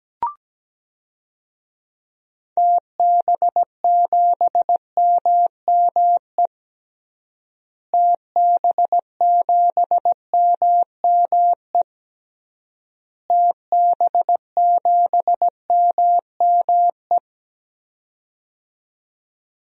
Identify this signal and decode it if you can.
Morse code, heard as a keyed tone, about 17 wpm, 700 Hz. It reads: TB7MME TB7MME TB7MME
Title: DXCC-100 Call Signs - 6 Characters - Repeated 3x - 17wpm